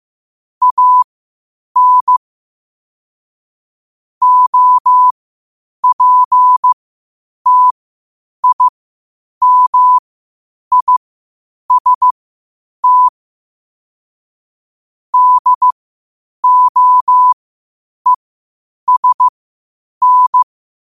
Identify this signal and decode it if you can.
Morse code, heard as a keyed tone, about 15 wpm, spaced as for 8 wpm, 1000 Hz. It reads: AN OPTIMIST DOESN
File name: qotdfiles / 8wpm